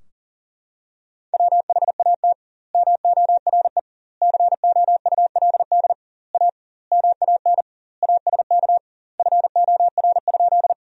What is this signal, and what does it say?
Morse code, heard as a keyed tone, about 40 wpm, 700 Hz. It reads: WHAT MORE COULD A MAN ASK FOR?
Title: 2x Word-Spacing - Sentences from Top 200 Words - 40wpm